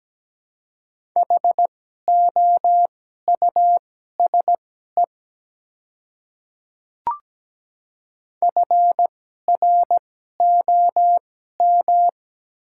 Morse code, heard as a keyed tone, about 17 wpm, spaced as for 12 wpm, 700 Hz. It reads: HOUSE FROM
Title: 2x Character Spacing - Top 200 Words - 17wpm